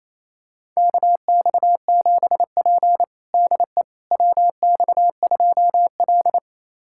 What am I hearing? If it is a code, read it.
Morse code, heard as a keyed tone, about 29 wpm, 690 Hz. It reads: KX7P DE WX2L
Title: USA Callsign DE Callsign - 4 Characters - 28wpm